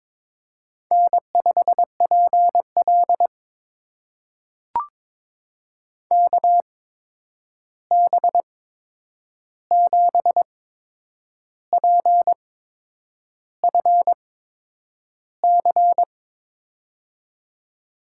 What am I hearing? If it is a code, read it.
Morse code, heard as a keyed tone, about 22 wpm, spaced as for 8 wpm, 700 Hz. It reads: N5PL KB7PFC
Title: USA Call Signs - 8x Character Spacing - 22wpm